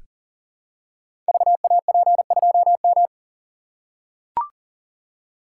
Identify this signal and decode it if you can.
Morse code, heard as a keyed tone, about 38 wpm, 700 Hz. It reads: VAP2M
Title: Code Groups - 5 Letter:Numbers - 40wpm